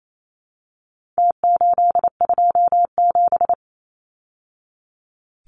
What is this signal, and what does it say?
Morse code, heard as a keyed tone, about 28 wpm, 700 Hz. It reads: T827
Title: Numbers - Contest Style - 4-Digit 28wpm